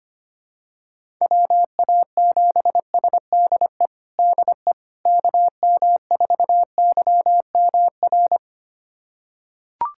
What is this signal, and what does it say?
Morse code, heard as a keyed tone, about 25 wpm, 700 Hz. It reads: WA7SDE DE KM4YMR
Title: USA Callsign DE Callsign - 6 Characters - 25wpm